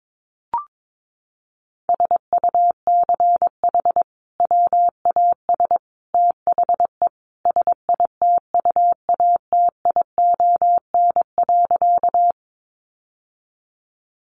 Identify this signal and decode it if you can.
Morse code, heard as a keyed tone, about 22 wpm, 700 Hz. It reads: SUCH WAS THE SITUATION.
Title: Sentences from Top 1200 Words 22wpm